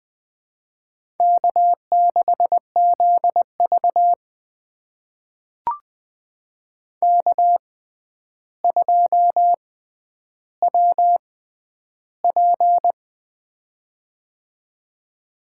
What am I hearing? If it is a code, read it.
Morse code, heard as a keyed tone, about 20 wpm, spaced as for 10 wpm, 700 Hz. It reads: K6ZV K2WP
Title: USA Call Signs - 4 Characters - 6x Character Spacing - 20wpm